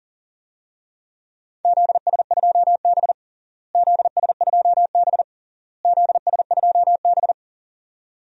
Morse code, heard as a keyed tone, about 40 wpm, 700 Hz. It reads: ZS2B ZS2B ZS2B
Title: DXCC-100 Call Signs - 4 Characters - Repeated 3x - 40wpm